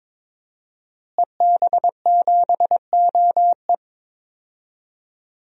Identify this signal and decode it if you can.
Morse code, heard as a keyed tone, about 22 wpm, 700 Hz. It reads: EB7OE
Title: DXCC-100 Call Signs - 5 Characters - Repeated 2x - 22wpm